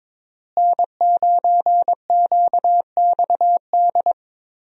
Morse code, heard as a keyed tone, about 22 wpm, 700 Hz. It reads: N9QXD
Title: USA Call Signs - 5 Characters - Repeated 2x - 22wpm